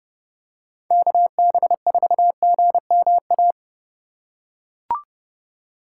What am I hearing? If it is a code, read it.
Morse code, heard as a keyed tone, about 29 wpm, 700 Hz. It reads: KB4GMA